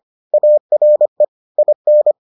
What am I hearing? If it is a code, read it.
Morse code, heard as a keyed tone, about 25 wpm, 600 Hz. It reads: ARE IN